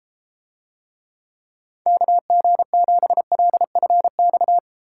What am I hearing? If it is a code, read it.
Morse code, heard as a keyed tone, about 33 wpm, 700 Hz. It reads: KG7LFX